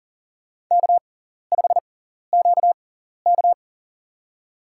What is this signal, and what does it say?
Morse code, heard as a keyed tone, about 40 wpm, spaced as for 14 wpm, 700 Hz. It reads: K5QK
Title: USA Call Signs - 4 Characters - 6x Character Spacing - 40wpm